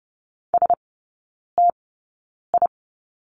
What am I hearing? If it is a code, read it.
Morse code, heard as a keyed tone, about 30 wpm, spaced as for 9 wpm, 700 Hz. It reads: STI
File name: 081 Lesson 14e - Number 2 - Three Characters 30wpm